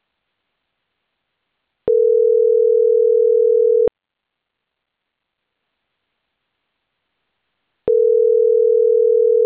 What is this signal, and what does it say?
Ringback tone (North America), 440 + 480 Hz.